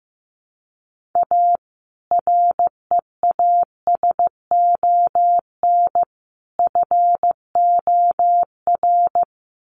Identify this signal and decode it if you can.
Morse code, heard as a keyed tone, about 15 wpm, 700 Hz. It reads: A REASON FOR
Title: Sentences from Top 1300 Words 15wpm